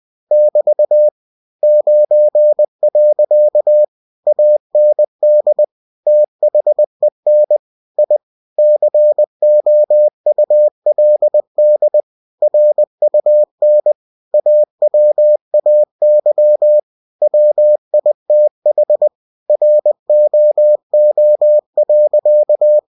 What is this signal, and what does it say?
Morse code, heard as a keyed tone, about 20 wpm, 600 Hz. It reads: <BT> 9. AND THEN I COULD RUN AWAY WITH ROO.
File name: cw-books / winnie-the-pooh / winnie-20wpm-page-0064